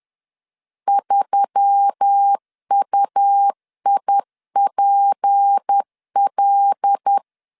Morse code, heard as a keyed tone, about 10 wpm, 800 Hz. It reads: 3UIPL
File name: LEZIONI / 6 PKX0 / 6 P 10wpm 2432